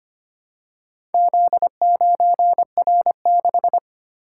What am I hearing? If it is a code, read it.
Morse code, heard as a keyed tone, about 25 wpm, 700 Hz. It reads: Z9R6
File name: practice / 25wpm / Four Letter-Number Combinations - 25wpm